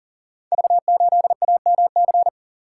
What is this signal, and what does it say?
Morse code, heard as a keyed tone, about 38 wpm, 690 Hz. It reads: V8AMC
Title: Code Groups - 5 Letter:Numbers - 40wpm